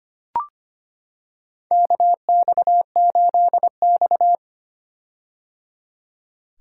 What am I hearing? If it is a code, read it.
Morse code, heard as a keyed tone, about 25 wpm, 700 Hz. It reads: KX8X
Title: K1USN SST QSO Practice - 25wpm